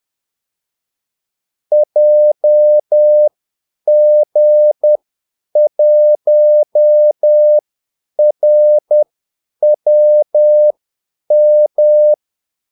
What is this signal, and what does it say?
Morse code, heard as a keyed tone, about 10 wpm, spaced as for 8 wpm, 600 Hz. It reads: JG1RWM
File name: Callsigns-10-8wpm